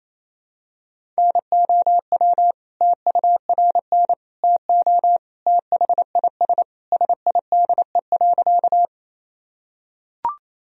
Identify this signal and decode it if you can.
Morse code, heard as a keyed tone, about 28 wpm, 700 Hz. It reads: NOW TURN TO THIS SIDE.
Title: Sentences from Top 1000 Words - Review Part 1 - 28wpm